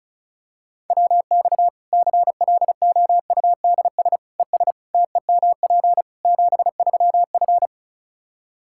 Morse code, heard as a keyed tone, about 34 wpm, 700 Hz. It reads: WX CLOUDS ES TEMP 73F
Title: Practice QSO - Exchange 2 - Info - 35wpm